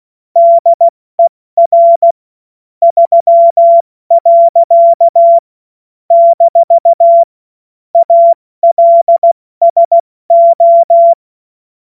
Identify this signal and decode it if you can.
Morse code, heard as a keyed tone, about 16 wpm, 680 Hz. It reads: DER 3. - ALSO
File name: Morsekurs / Witze / Witz 02 April 2024 70 F80